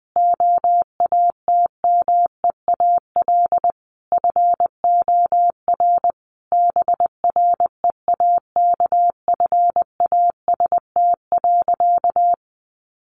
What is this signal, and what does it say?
Morse code, heard as a keyed tone, about 20 wpm, 700 Hz. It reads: OATMEAL FOR BREAKFAST.
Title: Easy Phrases 20wpm